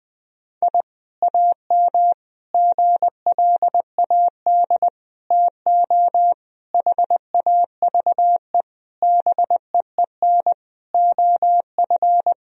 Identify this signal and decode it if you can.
Morse code, heard as a keyed tone, about 20 wpm, 700 Hz. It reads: I AM GLAD TO HAVE BEEN OF
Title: Sentences from Top 1000 Words - Review Part 4 - 20wpm